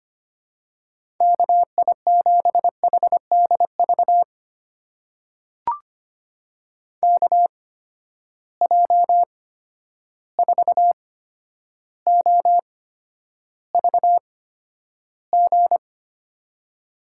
Morse code, heard as a keyed tone, about 25 wpm, spaced as for 11 wpm, 700 Hz. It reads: KI7HDV KJ4OVG
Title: USA Call Signs - 6 Characters - 8x Character Spacing - 25wpm